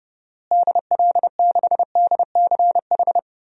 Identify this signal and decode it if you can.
Morse code, heard as a keyed tone, about 30 wpm, 700 Hz. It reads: DL6DCH
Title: DXCC-100 Call Signs - 6 Characters - Repeated 3x - 30wpm